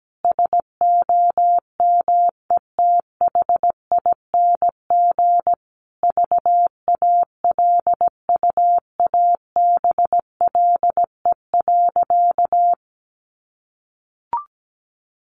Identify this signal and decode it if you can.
Morse code, heard as a keyed tone, about 17 wpm, 700 Hz. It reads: SOMETHING VALUABLE.
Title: Sentences from Top 1400 Words 17wpm